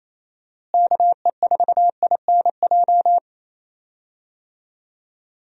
Morse code, heard as a keyed tone, about 27 wpm, 700 Hz. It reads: KE4INJ
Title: USA Call Signs - 6 Characters - Repeated 3x - 28wpm